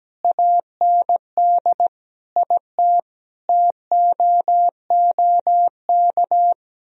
Morse code, heard as a keyed tone, about 17 wpm, 700 Hz. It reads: AND IT TOOK
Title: Sentences from Top 1000 Words - Review Part 4 - 17wpm